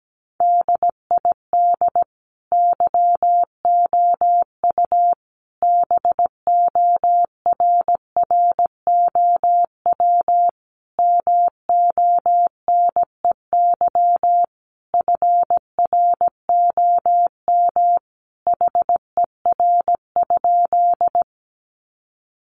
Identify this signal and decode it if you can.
Morse code, heard as a keyed tone, about 17 wpm, 700 Hz. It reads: DID YOU BORROW MONEY FROM HER?
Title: Sentences from Top 1500 Words 17wpm